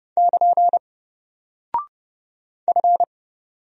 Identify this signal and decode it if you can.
Morse code, heard as a keyed tone, about 28 wpm, 700 Hz. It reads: <KN> F
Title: Single Character (Letter:Number:Punctuation:Prosign) 30wpm